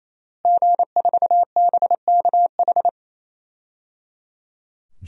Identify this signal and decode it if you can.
Morse code, heard as a keyed tone, about 27 wpm, 700 Hz. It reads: G4BKH
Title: DXCC-100 Call Signs - 5 Characters - Repeated 3x - 28wpm